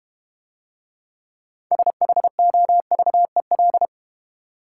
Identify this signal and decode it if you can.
Morse code, heard as a keyed tone, about 30 wpm, 700 Hz. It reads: SHOVEL
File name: Easy Words - Speed-Racing with Hints - 25wpm